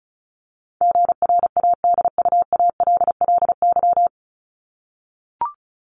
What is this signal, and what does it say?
Morse code, heard as a keyed tone, about 34 wpm, 700 Hz. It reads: GRADUALLY
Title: LY Suffix - Top 100 Words with Base Words 35wpm